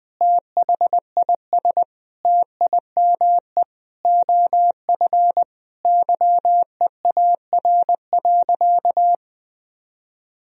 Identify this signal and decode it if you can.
Morse code, heard as a keyed tone, about 20 wpm, 700 Hz. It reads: THIS TIME OF YEAR.
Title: Sentences from Top 1000 Words - Review Part 3 - 20wpm